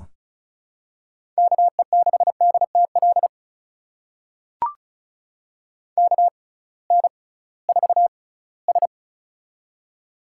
Morse code, heard as a keyed tone, about 36 wpm, spaced as for 20 wpm, 700 Hz. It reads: KE6DTL KN4S
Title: USA Call Signs - 6x Character Spacing - 35wpm